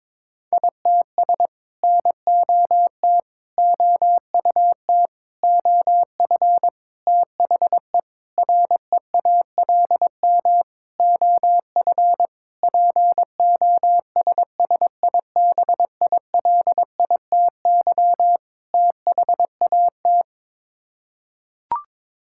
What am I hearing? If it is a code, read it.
Morse code, heard as a keyed tone, about 22 wpm, 700 Hz. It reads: ITS NOT OUT OF THE REALM OF POSSIBILITY THAT
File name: Sets of 9 Words - 22wpm